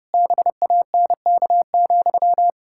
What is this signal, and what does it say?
Morse code, heard as a keyed tone, about 30 wpm, 700 Hz. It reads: BANK,
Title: Pangram Challenge v2 - 30wpm